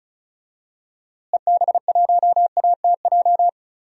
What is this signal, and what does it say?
Morse code, heard as a keyed tone, about 34 wpm, 700 Hz. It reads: EB1ATJ
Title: DXCC-100 Call Signs - 6 Characters - Repeated 2x - 35wpm